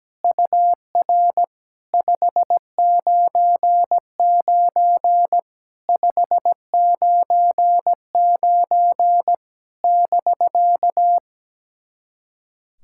Morse code, heard as a keyed tone, about 17 wpm, 700 Hz. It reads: UR 599 599 <BK>
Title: SOTA QSO Practice 17wpm